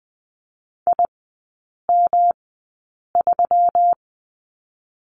Morse code, heard as a keyed tone, about 20 wpm, spaced as for 8 wpm, 690 Hz. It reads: IM3
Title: 133 Lesson 22e - Question Mark - Three Characters 20wpm